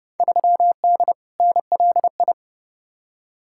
Morse code, heard as a keyed tone, about 31 wpm, 700 Hz. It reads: 3D NLI
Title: Field Day Exchanges Encore - 30wpm